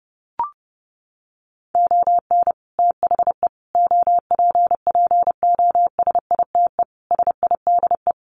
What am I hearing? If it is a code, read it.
Morse code, heard as a keyed tone, about 30 wpm, 700 Hz. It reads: ON THE OPPOSITE SIDE
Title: Sets of 5 Words Encore2 30wpm